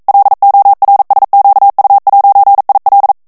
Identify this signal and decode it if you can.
Morse code, heard as a keyed tone, about 42 wpm, 800 Hz. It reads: LORSQU'IL
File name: CW / FICHIERS-MP3 / FRANCAIS TEXTE CLAIR 2 VITESSE 40 MOTS MINUTES